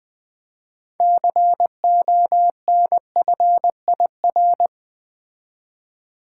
Morse code, heard as a keyed tone, about 20 wpm, 700 Hz. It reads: CONFIR
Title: Build A Word - 6 Letters At A Time - 12-Letter Words - 20wpm